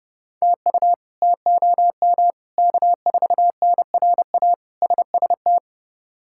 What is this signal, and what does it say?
Morse code, heard as a keyed tone, about 31 wpm, 700 Hz. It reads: TU TOM K4NRA SST
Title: K1USN SST QSO Practice - 30wpm